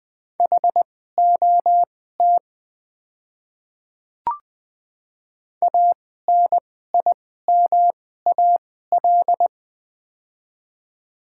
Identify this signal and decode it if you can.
Morse code, heard as a keyed tone, about 20 wpm, spaced as for 14 wpm, 700 Hz. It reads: HOT ANIMAL
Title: 2x Character Spacing - Top 200 Words - 20wpm